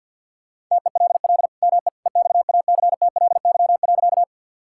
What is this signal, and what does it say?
Morse code, heard as a keyed tone, about 47 wpm, 700 Hz. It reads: TELL ME EXACTLY.
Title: Sentences from Top 1000 Words - Review Part 6 - 50wpm